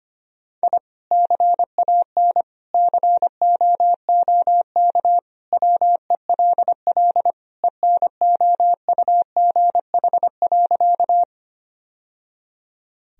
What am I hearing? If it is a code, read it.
Morse code, heard as a keyed tone, about 25 wpm, 700 Hz. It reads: I CAN COOK WELL ENOUGH.
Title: Sentences from Top 1000 Words - Review Part 4 - 25wpm